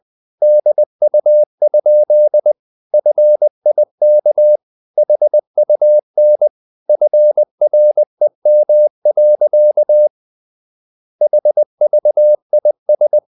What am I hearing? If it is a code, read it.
Morse code, heard as a keyed tone, about 20 wpm, 600 Hz. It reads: DU? FIK HUN FREM. HVIS
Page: Ferie 20 wpm